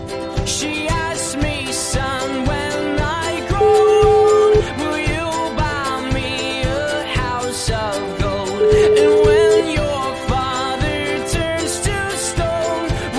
Ringback tone (Europe, 425 Hz), with noise.